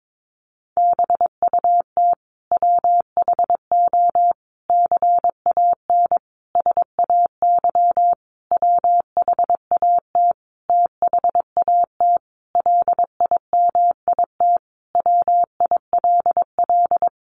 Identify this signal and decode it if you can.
Morse code, heard as a keyed tone, about 22 wpm, 700 Hz. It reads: BUT WHO CAN SAY WHAT THAT LIMIT WILL
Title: Sentences from Top 1500 Words 22wpm